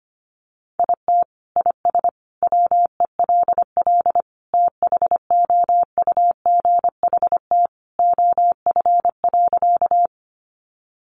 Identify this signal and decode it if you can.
Morse code, heard as a keyed tone, about 25 wpm, 700 Hz. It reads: IT IS WELL THOUGHT OF.
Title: Sentences from Top 300 Words 25wpm